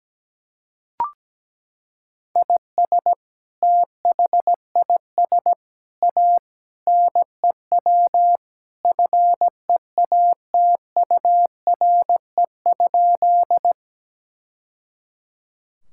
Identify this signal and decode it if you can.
Morse code, heard as a keyed tone, about 17 wpm, 700 Hz. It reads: IS THIS A NEW FEATURE?
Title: Sentences from Top 1000 Words - Review Part 5 - 17wpm